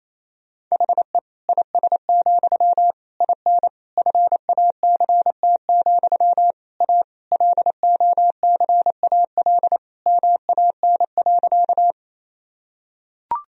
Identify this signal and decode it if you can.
Morse code, heard as a keyed tone, about 27 wpm, 700 Hz. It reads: HE IS, IN FACT, A LOCAL MAN.
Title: Sentences from Top 1000 Words - Review Part 3 - 28wpm